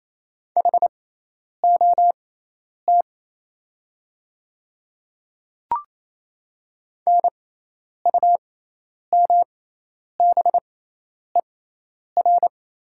Morse code, heard as a keyed tone, about 27 wpm, spaced as for 10 wpm, 700 Hz. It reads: HOT NUMBER
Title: 6x Character Spacing - Top 100 Words - 28wpm